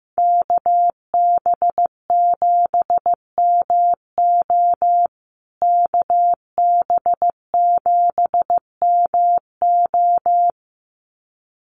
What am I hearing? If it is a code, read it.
Morse code, heard as a keyed tone, about 15 wpm, 700 Hz. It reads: KB7MO KB7MO
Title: USA Call Signs Repeated 2x 15wpm